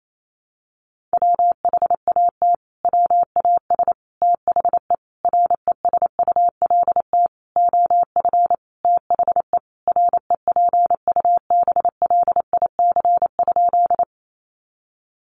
Morse code, heard as a keyed tone, about 28 wpm, 700 Hz. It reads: WHAT WAS THE RESULT OF THE REPUBLIC?
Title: Sentences from Top 1500 Words 28wpm